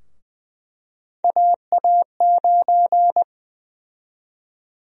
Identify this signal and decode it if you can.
Morse code, heard as a keyed tone, about 21 wpm, 700 Hz. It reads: AA9
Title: USA Call Sign Prefixes - Repeated 2x - 20wpm